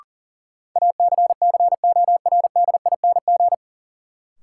Morse code, heard as a keyed tone, about 40 wpm, 700 Hz. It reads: ACCORDING
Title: Sets of 2 Words - New Word Drill - 40wpm